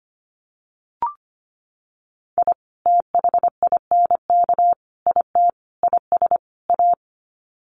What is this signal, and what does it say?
Morse code, heard as a keyed tone, about 25 wpm, 700 Hz. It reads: I THINK IT IS A
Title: Sets of 5 Words Encore 25wpm